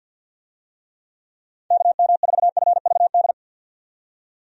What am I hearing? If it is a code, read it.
Morse code, heard as a keyed tone, about 50 wpm, 700 Hz. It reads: KM4FVD
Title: USA Call Signs - 6 Characters - 50wpm